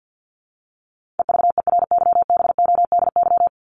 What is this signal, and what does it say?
Morse code, heard as a keyed tone, about 50 wpm, 700 Hz. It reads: EVERYBODY
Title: Top 1251-1500 Words 50wpm